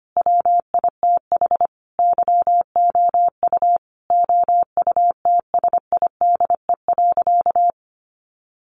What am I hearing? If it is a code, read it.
Morse code, heard as a keyed tone, about 25 wpm, 700 Hz. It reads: WITH YOU OUTSIDE.
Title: Sentences from Top 900 Words 25wpm